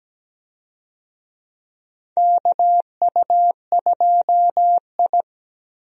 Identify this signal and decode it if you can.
Morse code, heard as a keyed tone, about 17 wpm, 700 Hz. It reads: KU2I
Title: USA Call Signs - 4 Characters - Repeated 2x - 17wpm